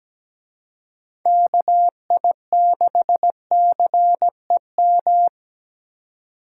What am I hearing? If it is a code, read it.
Morse code, heard as a keyed tone, about 20 wpm, 700 Hz. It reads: KI6CEM
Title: Speed Racing - USA 2x3 Call Signs 17wpm